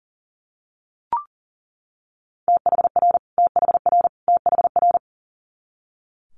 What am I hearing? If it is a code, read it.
Morse code, heard as a keyed tone, about 38 wpm, 700 Hz. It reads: THR THR THR
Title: Multi-letter Phonemes Repeated 3x 40wpm